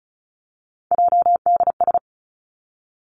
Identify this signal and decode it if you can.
Morse code, heard as a keyed tone, about 36 wpm, 700 Hz. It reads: JDS
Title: Three Challenging Letter-Number Combinations 35wpm